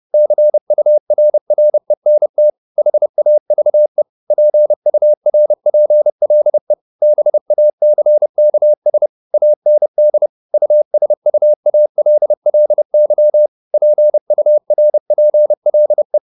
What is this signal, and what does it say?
Morse code, heard as a keyed tone, about 30 wpm, 600 Hz. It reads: CURRENT HAVE PURPLE BACKS AND USUALLY PURPLE
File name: the old man and the sea 2 30wpm